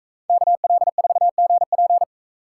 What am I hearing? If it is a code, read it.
Morse code, heard as a keyed tone, about 44 wpm, 700 Hz. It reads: KL4GP